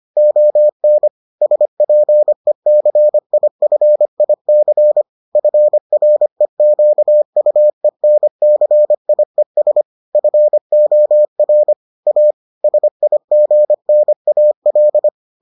Morse code, heard as a keyed tone, about 25 wpm, 600 Hz. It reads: ON SPECIFIC FREQUENCIES FOR A SIGNAL